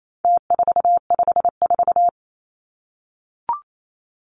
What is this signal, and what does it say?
Morse code, heard as a keyed tone, about 28 wpm, 700 Hz. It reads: T454
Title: Numbers - Contest Style - 4-Digit 28wpm